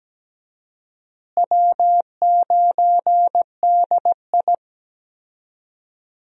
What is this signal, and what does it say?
Morse code, heard as a keyed tone, about 17 wpm, 700 Hz. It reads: W9DI
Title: USA Call Signs - 4 Characters - Repeated 3x - 17wpm